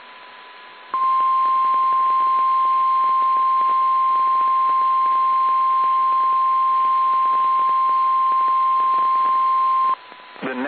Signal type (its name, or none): Emergency Alert System (SAME) tones